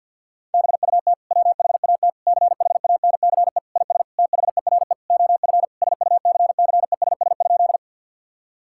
Morse code, heard as a keyed tone, about 50 wpm, 700 Hz. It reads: BUT WHAT CHANCE IS THERE OF SUCCESS?
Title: Sentences from Top 1000 Words - Review Part 6 - 50wpm